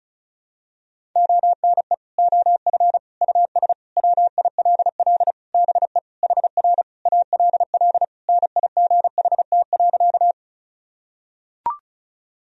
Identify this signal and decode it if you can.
Morse code, heard as a keyed tone, about 36 wpm, 700 Hz. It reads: ONE OF US WILL BE HR ALL NIGHT.
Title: Sentences from Top 300 Words and Abbreviations 35wpm